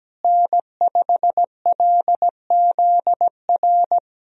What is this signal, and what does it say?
Morse code, heard as a keyed tone, about 17 wpm, 700 Hz. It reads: N5LZR
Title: USA Call Signs - 5 Characters - Repeated 3x - 17wpm